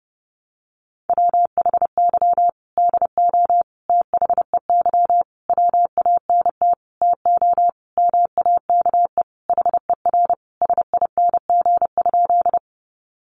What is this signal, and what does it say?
Morse code, heard as a keyed tone, about 30 wpm, 700 Hz. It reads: WHY DO THEY WANT TO MAKE HER SING?
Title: Sentences from Top 400 Words 30wpm